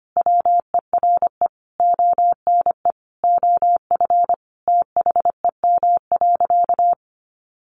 Morse code, heard as a keyed tone, about 25 wpm, 700 Hz. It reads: WERE ONE OF THEM.
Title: Sentences from Top 1500 Words 25wpm